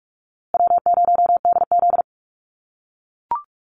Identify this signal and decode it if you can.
Morse code, heard as a keyed tone, about 44 wpm, 700 Hz. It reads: W0DZ
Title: USA Call Signs Repeated 2x 45wpm